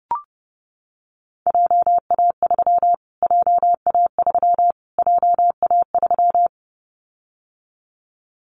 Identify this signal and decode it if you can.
Morse code, heard as a keyed tone, about 28 wpm, 700 Hz. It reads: JA3 JA3 JA3
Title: Callsign Prefixes for Top 25 DXCC - Repeated 3x - 30wpm